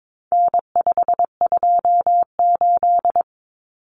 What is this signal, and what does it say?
Morse code, heard as a keyed tone, about 22 wpm, 700 Hz. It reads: N528
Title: Common Cut Numbers - 4-Digit 22wpm